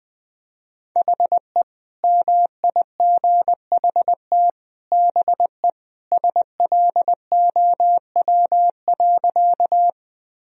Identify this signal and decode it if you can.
Morse code, heard as a keyed tone, about 21 wpm, 700 Hz. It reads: HE MIGHT BE SLOW.
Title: Sentences from Top 1000 Words - Review Part 2 - 20wpm